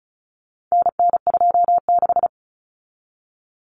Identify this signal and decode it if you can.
Morse code, heard as a keyed tone, about 34 wpm, 700 Hz. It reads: NN26
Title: Common Cut Numbers - 4-Digit 35wpm